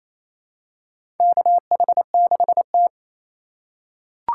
Morse code, heard as a keyed tone, about 28 wpm, 700 Hz. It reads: KH6T
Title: USA Call Signs - 4 Characters - 2x Character Spacing - 28wpm